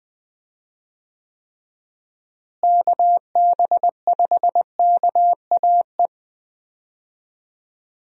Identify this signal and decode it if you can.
Morse code, heard as a keyed tone, about 21 wpm, 700 Hz. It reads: KB5KAE
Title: USA Call Signs - 6 Characters - Repeated 2x - 20wpm